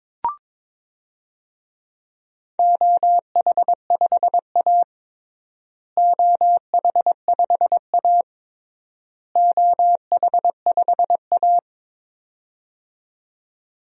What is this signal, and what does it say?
Morse code, heard as a keyed tone, about 22 wpm, 700 Hz. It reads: OH5A OH5A OH5A
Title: DXCC-100 Call Signs - 4 Characters - Repeated 3x - 22wpm